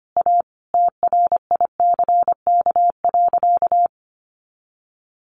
Morse code, heard as a keyed tone, about 25 wpm, 700 Hz. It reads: A TRICK.
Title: Sentences from Top 1500 Words 25wpm